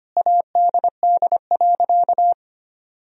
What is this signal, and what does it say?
Morse code, heard as a keyed tone, about 25 wpm, 700 Hz. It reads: ADD.
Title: Sentences from Top 1000 Words - Review Part 1 - 25wpm